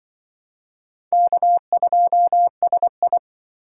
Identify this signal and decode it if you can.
Morse code, heard as a keyed tone, about 25 wpm, 700 Hz. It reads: K2SI